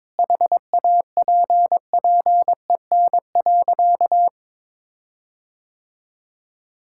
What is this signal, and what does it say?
Morse code, heard as a keyed tone, about 22 wpm, 700 Hz. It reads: HAPPEN.
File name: Sentences with 9-Word Phrases - 22wpm